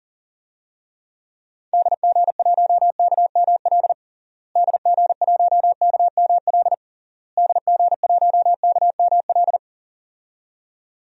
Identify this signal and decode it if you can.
Morse code, heard as a keyed tone, about 40 wpm, 700 Hz. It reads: DG1KML DG1KML DG1KML
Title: DXCC-100 Call Signs - 6 Characters - Repeated 3x - 40wpm